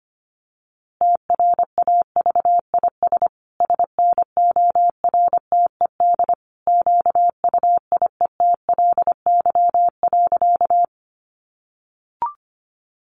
Morse code, heard as a keyed tone, about 25 wpm, 700 Hz. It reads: TRAVIS SNORTED QUIETLY.
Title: A Sound Of Thunder 25wpm